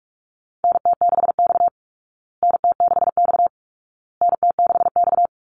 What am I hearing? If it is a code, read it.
Morse code, heard as a keyed tone, about 44 wpm, 700 Hz. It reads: NT6X NT6X NT6X
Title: Call Signs Repeated 2x - 45wpm